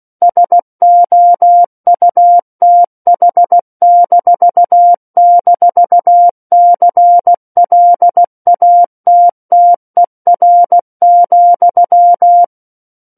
Morse code, heard as a keyed tone, about 16 wpm, 700 Hz. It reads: SOUTH--CLATTER,
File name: Book One / 07-Chapter 7 - 16 WPM